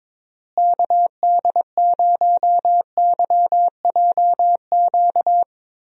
Morse code, heard as a keyed tone, about 23 wpm, 700 Hz. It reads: KD0YJQ
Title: USA Call Signs - 6 Characters - Repeated 3x - 22wpm